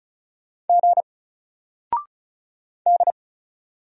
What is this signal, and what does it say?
Morse code, heard as a keyed tone, about 35 wpm, 700 Hz. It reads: G D
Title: Single Character (Letter:Number:Punctuation:Prosign) 35wpm